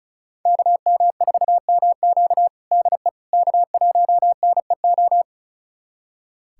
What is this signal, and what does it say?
Morse code, heard as a keyed tone, about 36 wpm, 700 Hz. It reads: KM4MQ DE K1NEO
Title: USA Callsign DE Callsign - 5 Characters - 35wpm